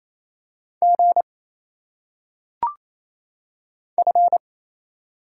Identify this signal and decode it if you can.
Morse code, heard as a keyed tone, about 27 wpm, 700 Hz. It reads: G F
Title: Single Letters - Phonetic Pronunciation 28wpm